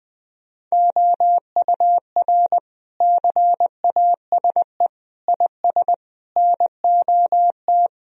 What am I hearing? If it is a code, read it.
Morse code, heard as a keyed tone, about 21 wpm, 700 Hz. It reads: OUR CASE IS NOT
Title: Sentences from Top 1000 Words - Review Part 2 - 20wpm